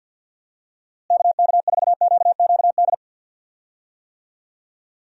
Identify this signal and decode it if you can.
Morse code, heard as a keyed tone, about 48 wpm, 700 Hz. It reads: KK4QQD
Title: USA Call Signs - 6 Characters - Repeated 2x - 50wpm